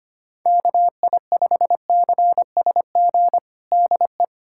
Morse code, heard as a keyed tone, about 25 wpm, 700 Hz. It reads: KI5CSG DE
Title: USA Callsign DE Callsign - 6 Characters - 25wpm